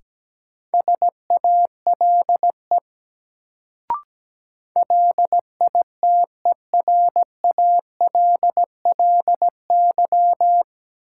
Morse code, heard as a keyed tone, about 17 wpm, 700 Hz. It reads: SALE LITERALLY
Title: Sentences from Top 1400 Words - New Word Drill - 17wpm